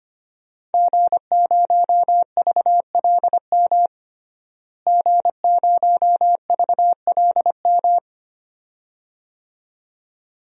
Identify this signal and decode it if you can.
Morse code, heard as a keyed tone, about 25 wpm, 700 Hz. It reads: G0VLM G0VLM
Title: DXCC-100 Call Signs - 5 Characters - Repeated 2x - 25wpm